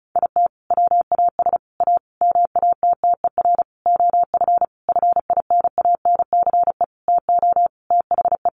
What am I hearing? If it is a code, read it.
Morse code, heard as a keyed tone, about 34 wpm, 700 Hz. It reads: IT WAS A MATTER OF FINANCE TO THE
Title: Sentences from Top 1500 Words 35wpm